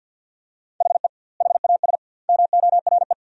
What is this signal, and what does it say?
Morse code, heard as a keyed tone, about 50 wpm, 700 Hz. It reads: HE HAS MORE
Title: 2x Word-Spacing - Sentences from Top 200 Words - 50wpm